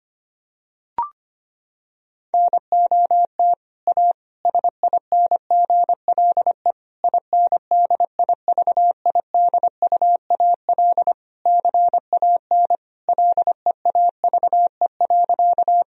Morse code, heard as a keyed tone, about 25 wpm, 700 Hz. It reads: NOT A SINGLE INDIVIDUAL CAN LEAVE.
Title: Sentences from Top 1000 Words - Review Part 4 - 25wpm